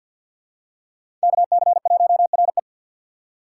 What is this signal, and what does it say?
Morse code, heard as a keyed tone, about 50 wpm, 700 Hz. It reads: KC1RE
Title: USA Call Signs - 5 Characters - 2x Character Spacing - 50wpm